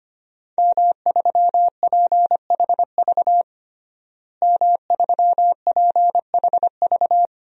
Morse code, heard as a keyed tone, about 25 wpm, 700 Hz. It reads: M3PHV M3PHV
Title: DXCC-100 Call Signs - 5 Characters - Repeated 2x - 25wpm